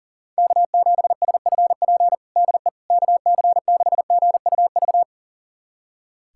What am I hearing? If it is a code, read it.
Morse code, heard as a keyed tone, about 40 wpm, 700 Hz. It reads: K7SFP DE KC6GUV
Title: USA Callsign DE Callsign 40wpm